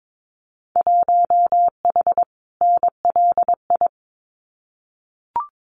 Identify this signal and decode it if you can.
Morse code, heard as a keyed tone, about 22 wpm, 700 Hz. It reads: 1H NLI
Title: Winter Field Day Exchanges 22wpm